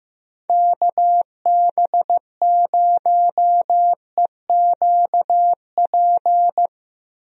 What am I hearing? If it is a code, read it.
Morse code, heard as a keyed tone, about 15 wpm, 700 Hz. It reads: KB0EQP
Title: USA Call Signs - 6 Characters - Repeated 2x - 15wpm